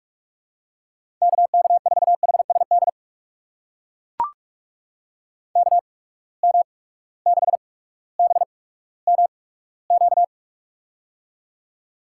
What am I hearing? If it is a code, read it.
Morse code, heard as a keyed tone, about 44 wpm, spaced as for 19 wpm, 700 Hz. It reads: KK4HSD KM6BMQ
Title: USA Call Signs - 8x Character Spacing - 45wpm